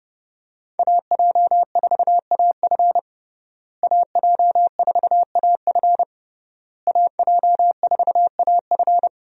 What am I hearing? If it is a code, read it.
Morse code, heard as a keyed tone, about 30 wpm, 700 Hz. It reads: AJ4AF AJ4AF AJ4AF
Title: USA Call Signs - 5 Characters - Repeated 3x - 30wpm